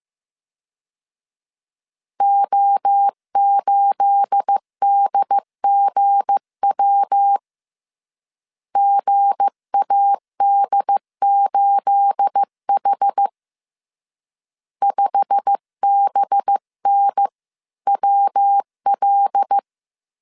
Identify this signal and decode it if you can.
Morse code, heard as a keyed tone, about 15 wpm, 800 Hz. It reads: O8DGW GAD8H 5BNWL
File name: CW / DET / 1 9 1500C S C 14wpm 3245